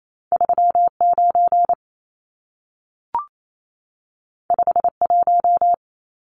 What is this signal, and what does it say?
Morse code, heard as a keyed tone, about 28 wpm, 700 Hz. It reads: 39 51
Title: Numbers - 2-Digit 28wpm